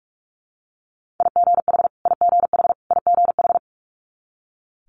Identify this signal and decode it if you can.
Morse code, heard as a keyed tone, about 46 wpm, 690 Hz. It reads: IGH IGH IGH
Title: Multi-letter Phonemes Repeated 3x 45wpm